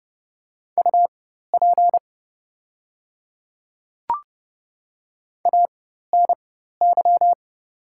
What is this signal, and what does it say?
Morse code, heard as a keyed tone, about 30 wpm, spaced as for 14 wpm, 700 Hz. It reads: UP ANY
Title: 4x Character Spacing - Top 200 Words - 30wpm